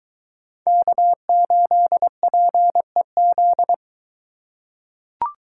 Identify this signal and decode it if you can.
Morse code, heard as a keyed tone, about 22 wpm, 700 Hz. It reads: K8PEZ